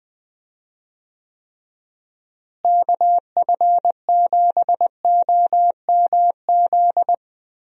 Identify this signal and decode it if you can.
Morse code, heard as a keyed tone, about 20 wpm, 700 Hz. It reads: KF7OMZ
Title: USA Call Signs - 6 Characters - Repeated 2x - 20wpm